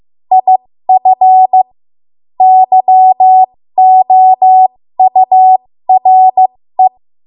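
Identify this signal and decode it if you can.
Morse code, heard as a keyed tone, about 15 wpm, spaced as for 13 wpm, 750 Hz. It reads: IF YOURE
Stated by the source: 13 wpm Morse code practice archive transmission.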